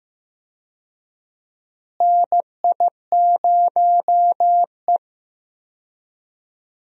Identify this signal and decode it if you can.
Morse code, heard as a keyed tone, about 15 wpm, 700 Hz. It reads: NI0E
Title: USA Call Signs - 4 Characters - Repeated 2x - 15wpm